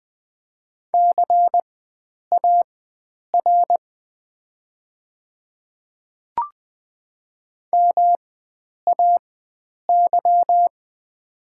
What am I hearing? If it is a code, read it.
Morse code, heard as a keyed tone, about 20 wpm, spaced as for 9 wpm, 700 Hz. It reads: CAR MAY
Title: 4x Character Spacing - Top 300 Words - 20wpm